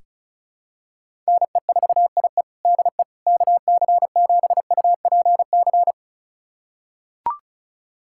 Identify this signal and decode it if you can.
Morse code, heard as a keyed tone, about 36 wpm, 700 Hz. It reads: NE4IE DE KC7UPC
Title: USA Callsign DE Callsign 35wpm